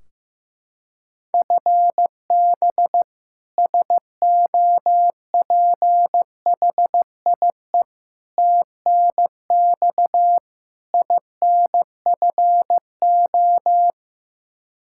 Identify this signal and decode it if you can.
Morse code, heard as a keyed tone, about 15 wpm, 700 Hz. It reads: FB SOPHIE TNX INFO
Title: Practice QSO - Exchange 3 - More Info - 15wpm